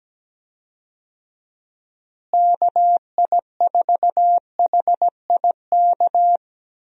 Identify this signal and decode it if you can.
Morse code, heard as a keyed tone, about 17 wpm, 700 Hz. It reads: KI4HIK
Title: USA Call Signs - 6 Characters - Repeated 3x - 17wpm